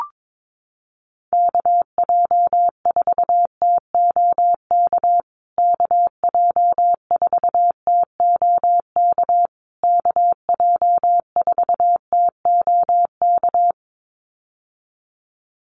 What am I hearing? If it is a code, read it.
Morse code, heard as a keyed tone, about 22 wpm, 700 Hz. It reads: KJ4TOK KJ4TOK KJ4TOK
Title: USA Call Signs Repeated 3x 22wpm